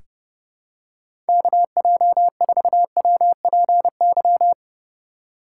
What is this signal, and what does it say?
Morse code, heard as a keyed tone, about 30 wpm, 700 Hz. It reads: KJ4WPY
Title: USA Call Signs - 6 Characters - 3x Character Spacing - 30wpm